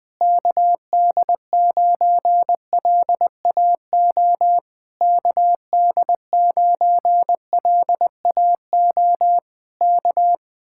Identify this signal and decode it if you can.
Morse code, heard as a keyed tone, about 20 wpm, 700 Hz. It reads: KD9LAO KD9LAO K
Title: Speed-Racing Common QSO Phrases - 20wpm